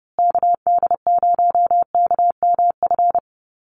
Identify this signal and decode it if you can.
Morse code, heard as a keyed tone, about 30 wpm, 700 Hz. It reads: KD0KMF
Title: USA Call Signs Repeated 2x 30wpm